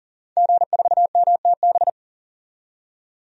Morse code, heard as a keyed tone, about 37 wpm, 700 Hz. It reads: G4MTB
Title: DXCC-100 Call Signs - 5 Characters - 40wpm